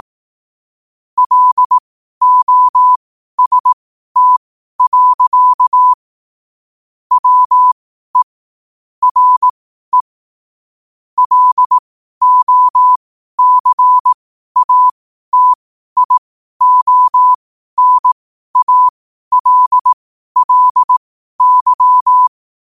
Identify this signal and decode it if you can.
Morse code, heard as a keyed tone, about 18 wpm, spaced as for 12 wpm, 1000 Hz. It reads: LOST. WE RE LOCATIONALLY
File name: qotdfiles / 12wpm